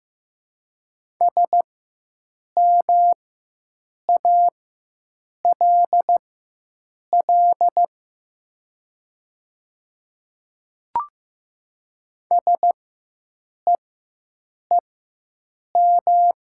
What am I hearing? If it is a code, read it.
Morse code, heard as a keyed tone, about 15 wpm, spaced as for 7 wpm, 700 Hz. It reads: SMALL SEEM
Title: 4x Character Spacing - Top 300 Words - 15wpm